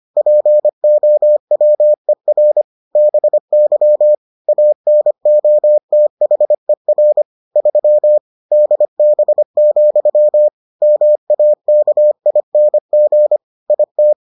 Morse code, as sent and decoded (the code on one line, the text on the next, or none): .--. --- .-- . .-. / -... -.-- / .- -. --- - .... .-. / ...-- / -.. -... --..-- / -- .- -.- .. -. --. / .. -
POWER BY ANOTHER 3 DB, MAKING IT